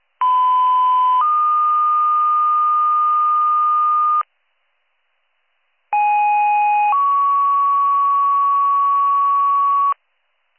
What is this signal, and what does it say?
Two-tone page, heard as a one-second tone, then a three-second tone, about 1010 then 1220 Hz, about 830 then 1120 Hz. It is heard from 0.2 to 9.9 s.